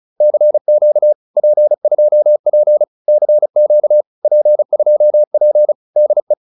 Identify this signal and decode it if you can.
Morse code, heard as a keyed tone, about 34 wpm, 590 Hz. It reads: CQ P2P CQ P2P DE
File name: POTA306 35